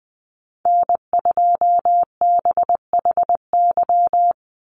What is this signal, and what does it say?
Morse code, heard as a keyed tone, about 20 wpm, 700 Hz. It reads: N2BHY
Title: USA Call Signs 20wpm